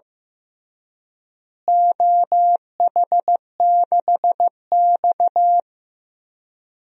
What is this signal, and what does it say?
Morse code, heard as a keyed tone, about 15 wpm, 700 Hz. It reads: OH6X
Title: DXCC-100 Call Signs - 4 Characters - Repeated 3x - 15wpm